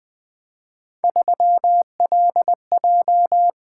Morse code, heard as a keyed tone, about 20 wpm, 700 Hz. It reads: 3LJ
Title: Three Letter-Number Combinations - 20wpm